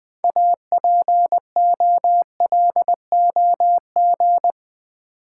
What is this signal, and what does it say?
Morse code, heard as a keyed tone, about 20 wpm, 700 Hz. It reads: APOLOG
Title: Build A Word - 3 Letters At A Time - 9-Letter Words - 20wpm